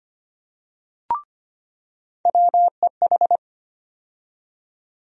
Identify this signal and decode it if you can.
Morse code, heard as a keyed tone, about 25 wpm, 700 Hz. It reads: WEH
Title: USA Call Sign Postfixes - 3 Letters 25wpm